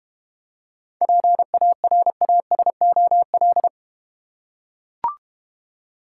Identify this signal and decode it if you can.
Morse code, heard as a keyed tone, about 30 wpm, 700 Hz. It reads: PARASOL
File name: Easy Words - Speed-Racing with Hints - 25wpm